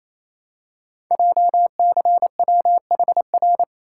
Morse code, heard as a keyed tone, about 27 wpm, 700 Hz. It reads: JCWHR
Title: Code Groups - 5 Letters - 28wpm